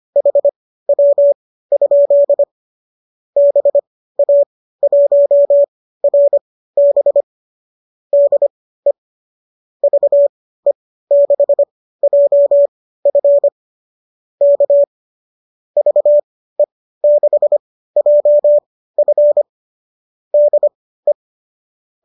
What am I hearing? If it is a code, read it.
Morse code, heard as a keyed tone, about 25 wpm, spaced as for 15 wpm, 570 Hz. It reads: HW? BA1RB DE VE6JF K VE6JF DE
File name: qso203 15